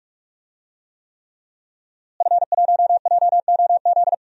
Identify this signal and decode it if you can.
Morse code, heard as a keyed tone, about 48 wpm, 700 Hz. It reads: F1JOZ
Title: DXCC-100 Call Signs - 5 Characters - Repeated 3x - 45wpm